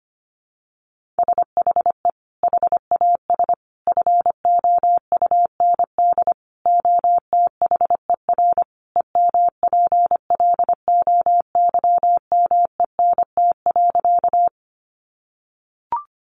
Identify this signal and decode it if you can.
Morse code, heard as a keyed tone, about 25 wpm, 700 Hz. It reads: SHE HAS FOUND OTHER EMPLOYMENT.
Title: Sentences from Top 1500 Words 25wpm